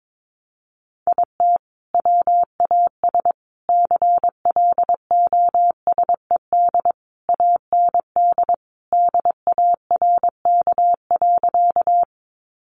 Morse code, heard as a keyed tone, about 22 wpm, 700 Hz. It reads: IT WAS CLOSED AND DARK.
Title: Sentences from Top 1400 Words 22wpm